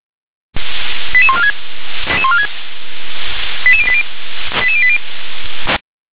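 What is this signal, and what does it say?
Five-tone selective call (ZVEI), heard as quick runs of five tones, tones of about 70 ms. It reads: 88146 88080 88080
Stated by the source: ZVEI five-tone selective call 88154 received by SDR.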